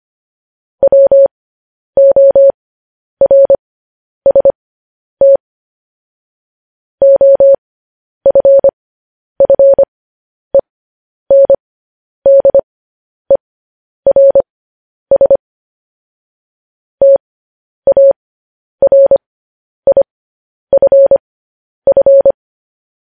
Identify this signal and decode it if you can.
Morse code, heard as a keyed tone, about 25 wpm, spaced as for 10 wpm, 550 Hz. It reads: WORST OFFENDERS TARIFF
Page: news headlines in morse code at 10 wpm